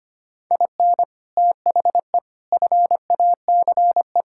Morse code, heard as a keyed tone, about 25 wpm, 700 Hz. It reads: IN THE FACE
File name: practice / 25wpm / Sentences with 3-Word Phrases - 25wpm